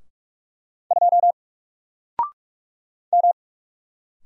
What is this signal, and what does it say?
Morse code, heard as a keyed tone, about 44 wpm, 700 Hz. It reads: 2 M
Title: Single Letter-Number Alt 45wpm